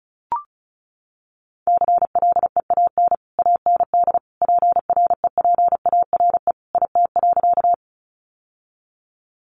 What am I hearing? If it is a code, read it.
Morse code, heard as a keyed tone, about 34 wpm, 700 Hz. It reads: CLEAN AND PREPARE IT.
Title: Sentences from Top 1100 Words 35wpm